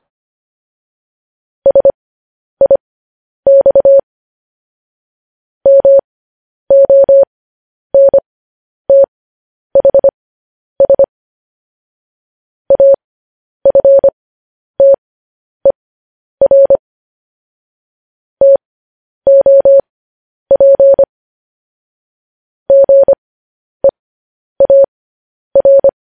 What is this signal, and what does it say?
Morse code, heard as a keyed tone, about 25 wpm, spaced as for 10 wpm, 550 Hz. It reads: SIX MONTHS AFTER TOP GEAR
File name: podcast / News headlines-1x-25x10